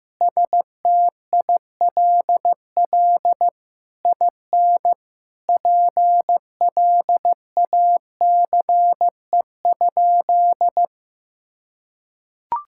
Morse code, heard as a keyed tone, about 16 wpm, 700 Hz. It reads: STILL IN PLACE?
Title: Sentences from Top 1000 Words - Review Part 6 - 15wpm